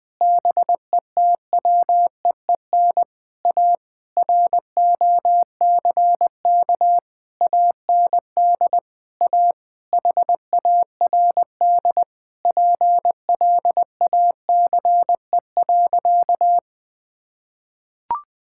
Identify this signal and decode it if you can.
Morse code, heard as a keyed tone, about 20 wpm, 700 Hz. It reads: BETWEEN A ROCK AND A HARD PLACE.
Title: Sentences with 8-Word Phrases - 20wpm